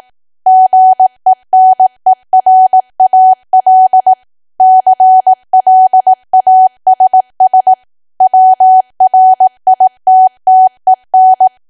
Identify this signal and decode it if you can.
Morse code, heard as a keyed tone, about 18 wpm, 750 Hz. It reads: GENERAL CLASS WRITTEN
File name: cwa / W1AW-Level2 / 18WPM